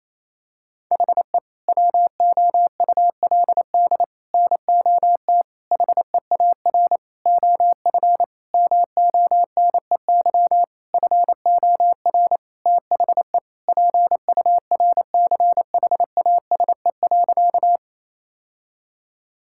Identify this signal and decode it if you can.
Morse code, heard as a keyed tone, about 28 wpm, 700 Hz. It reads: HE WOULD NOT HEAR OF MONEY FOR THE PURCHASE.
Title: Sentences from Top 1000 Words - Review Part 5 - 28wpm